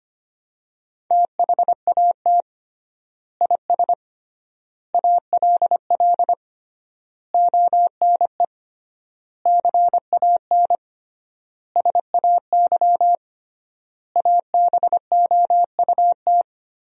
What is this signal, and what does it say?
Morse code, heard as a keyed tone, about 25 wpm, 700 Hz. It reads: THAT IS ALL ONE CAN SAY ABOUT
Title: 3x Word-Spacing - Sentences from Top 200 Words - 25wpm